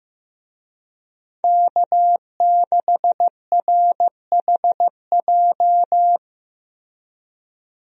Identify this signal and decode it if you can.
Morse code, heard as a keyed tone, about 15 wpm, 700 Hz. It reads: K6RHJ
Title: USA Call Signs - 5 Characters - Repeated 2x - 15wpm